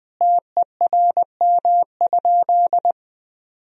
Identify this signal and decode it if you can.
Morse code, heard as a keyed tone, about 20 wpm, 700 Hz. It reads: TERM?
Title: Sentences from Top 1000 Words - Review Part 4 - 20wpm